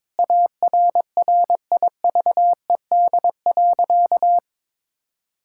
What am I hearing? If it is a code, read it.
Morse code, heard as a keyed tone, about 22 wpm, 700 Hz. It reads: ARRIVED.
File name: Sentences with 7-Word Phrases - 22wpm